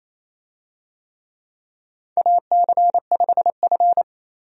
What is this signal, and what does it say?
Morse code, heard as a keyed tone, about 28 wpm, 700 Hz. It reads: AC5F
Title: USA Call Signs - 4 Characters - 28wpm